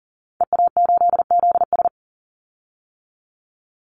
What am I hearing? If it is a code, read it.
Morse code, heard as a keyed tone, about 40 wpm, 700 Hz. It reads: EA8ZS
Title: Call Signs - 40wpm